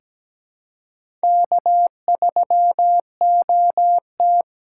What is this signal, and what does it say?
Morse code, heard as a keyed tone, about 17 wpm, 700 Hz. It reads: K3OT
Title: USA Call Signs - 4 Characters - 3x Character Spacing - 17wpm